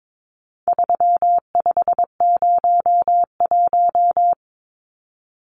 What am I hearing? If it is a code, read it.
Morse code, heard as a keyed tone, about 22 wpm, 700 Hz. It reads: 3501
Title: Numbers - 4-Digit 22wpm